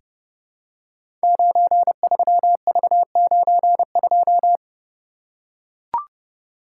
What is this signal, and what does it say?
Morse code, heard as a keyed tone, about 30 wpm, 700 Hz. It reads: 93V92
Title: Code Groups - 5 Letter:Numbers - 30wpm